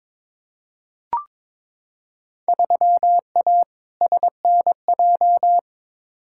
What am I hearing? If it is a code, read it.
Morse code, heard as a keyed tone, about 22 wpm, 700 Hz. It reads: 3A SNJ
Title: Field Day Exchanges - 22wpm